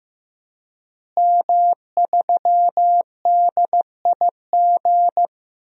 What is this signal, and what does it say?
Morse code, heard as a keyed tone, about 15 wpm, 700 Hz. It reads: M3DIG
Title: DXCC-100 Call Signs - 5 Characters - Repeated 3x - 15wpm